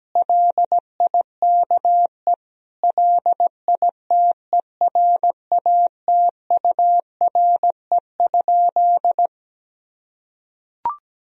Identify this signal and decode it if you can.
Morse code, heard as a keyed tone, about 17 wpm, 700 Hz. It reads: LIKE LITERATURE?
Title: Sentences from Top 1000 Words - Review Part 3 - 17wpm